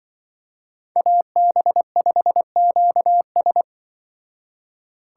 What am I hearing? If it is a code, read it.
Morse code, heard as a keyed tone, about 24 wpm, 700 Hz. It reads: AB5QS
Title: Speed Racing - USA 2x2 Call Signs 20wpm